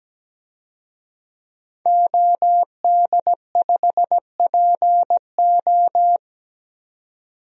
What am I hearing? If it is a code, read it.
Morse code, heard as a keyed tone, about 17 wpm, 700 Hz. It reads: OD5PO
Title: DXCC-100 Call Signs - 5 Characters - 17wpm